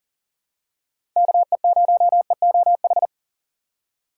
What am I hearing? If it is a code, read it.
Morse code, heard as a keyed tone, about 39 wpm, 700 Hz. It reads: KE0EOH
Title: USA Call Signs - 6 Characters - 3x Character Spacing - 40wpm